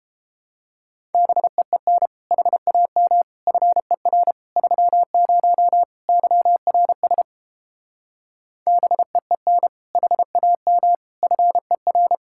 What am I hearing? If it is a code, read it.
Morse code, heard as a keyed tone, about 29 wpm, 700 Hz. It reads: BEEN HAM FER 30 YRS BEEN HAM FER